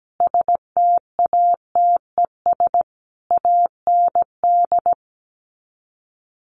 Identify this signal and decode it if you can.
Morse code, heard as a keyed tone, about 17 wpm, 700 Hz. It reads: STATES AND
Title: Sets of 5 Words Encore 17wpm